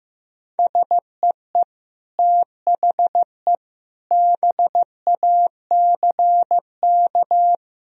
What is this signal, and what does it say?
Morse code, heard as a keyed tone, about 15 wpm, 700 Hz. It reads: SEE THE BACK
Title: Sentences from Top 1000 Words - Review Part 1 - 15wpm